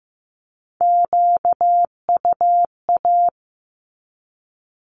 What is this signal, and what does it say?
Morse code, heard as a keyed tone, about 15 wpm, 700 Hz. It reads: QUA
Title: Q-Codes with Spoken Meaning - 15wpm